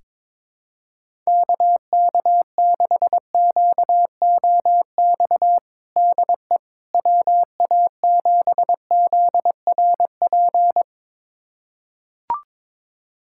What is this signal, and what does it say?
Morse code, heard as a keyed tone, about 22 wpm, 700 Hz. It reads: KK6QOX DE WA7ZRP
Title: USA Callsign DE Callsign - 6 Characters - 22wpm